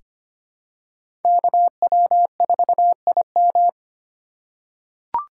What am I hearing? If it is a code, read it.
Morse code, heard as a keyed tone, about 25 wpm, 700 Hz. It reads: KW4IM
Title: USA Call Signs - 5 Characters - 4x Character Spacing - 25wpm